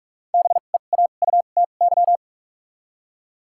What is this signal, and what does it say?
Morse code, heard as a keyed tone, about 46 wpm, spaced as for 32 wpm, 700 Hz. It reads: BEAUTY
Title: 2x Character Spacing - Top 500 Words - 45wpm